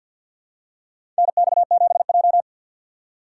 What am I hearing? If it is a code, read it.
Morse code, heard as a keyed tone, about 48 wpm, 700 Hz. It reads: NX7J
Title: USA Call Signs - 4 Characters - 3x Character Spacing - 50wpm